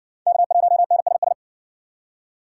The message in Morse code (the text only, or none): K1NSS